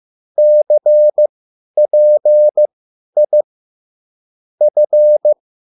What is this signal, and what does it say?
Morse code, heard as a keyed tone, about 15 wpm, spaced as for 10 wpm, 600 Hz. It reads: CPI F